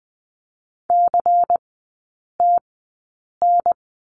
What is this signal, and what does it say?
Morse code, heard as a keyed tone, about 20 wpm, spaced as for 8 wpm, 700 Hz. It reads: CTN
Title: 093 Lesson 16e - Letter c - Three Characters 20wpm